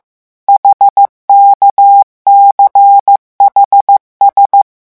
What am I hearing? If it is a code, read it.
Morse code, heard as a keyed tone, about 15 wpm, 800 Hz. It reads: HKCHS
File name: morse mp3 / revision7-vitesse 14 mots